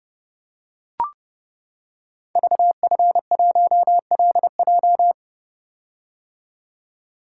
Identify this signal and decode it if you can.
Morse code, heard as a keyed tone, about 30 wpm, 700 Hz. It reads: VF1LJ
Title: Code Groups - 5 Letter:Numbers - 30wpm